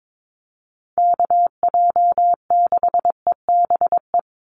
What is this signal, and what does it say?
Morse code, heard as a keyed tone, about 22 wpm, 700 Hz. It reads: KJ6EBE
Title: USA Call Signs 22wpm